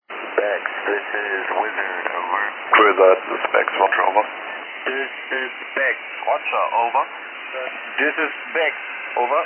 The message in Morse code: none

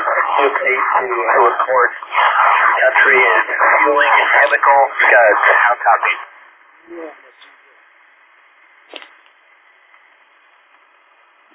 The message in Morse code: none

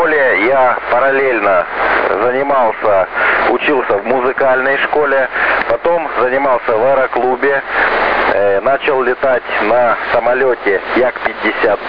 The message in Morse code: none

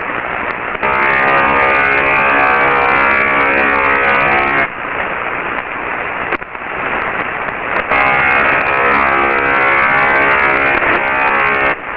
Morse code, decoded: none

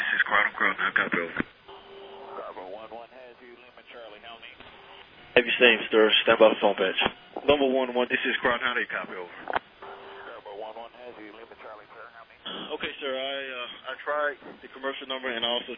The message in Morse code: none

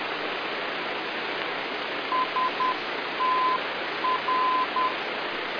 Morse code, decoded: STR

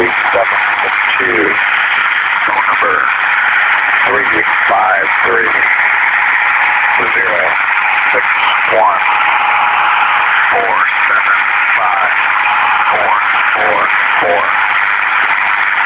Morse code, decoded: none

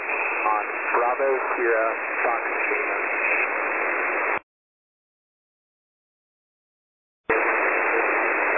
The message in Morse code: none

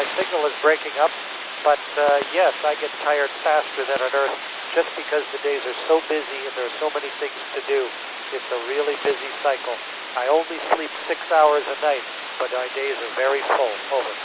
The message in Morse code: none

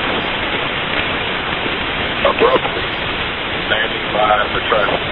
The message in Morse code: none

